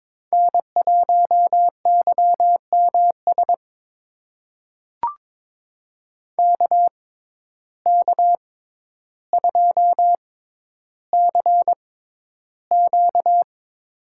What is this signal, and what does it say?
Morse code, heard as a keyed tone, about 22 wpm, spaced as for 11 wpm, 700 Hz. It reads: N1YMS KK2CQ
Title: USA Call Signs - 5 Characters - 6x Character Spacing - 22wpm